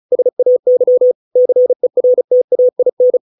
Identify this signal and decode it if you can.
Morse code, heard as a keyed tone, about 34 wpm, 500 Hz. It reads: SAY CERTAIN